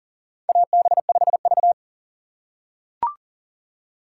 Morse code, heard as a keyed tone, about 39 wpm, 690 Hz. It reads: AB5V
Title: USA Call Signs - 4 Characters - 2x Character Spacing - 40wpm